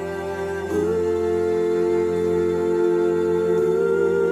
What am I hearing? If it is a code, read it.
Dial tone (North America), 350 + 440 Hz, with noise.
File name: wrct-log hi- 09